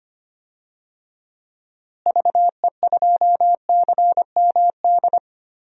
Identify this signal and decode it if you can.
Morse code, heard as a keyed tone, about 25 wpm, 700 Hz. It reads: VE2CMD
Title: DXCC-100 Call Signs - 6 Characters - Repeated 3x - 25wpm